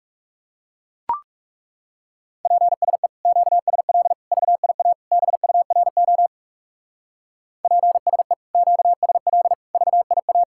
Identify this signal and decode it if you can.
Morse code, heard as a keyed tone, about 36 wpm, 700 Hz. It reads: PSE QSL VIA BURO PSE QSL VIA
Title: Speed-Racing Common QSO Phrases Encore - 30wpm